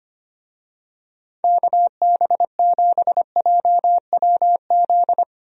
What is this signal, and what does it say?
Morse code, heard as a keyed tone, about 25 wpm, 700 Hz. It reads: KB7JWZ
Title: USA Call Signs - 6 Characters - 25wpm